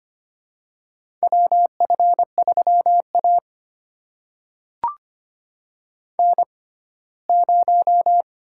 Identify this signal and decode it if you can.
Morse code, heard as a keyed tone, about 25 wpm, spaced as for 17 wpm, 700 Hz. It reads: WF3A N0
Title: USA Call Signs - 6x Character Spacing - 25wpm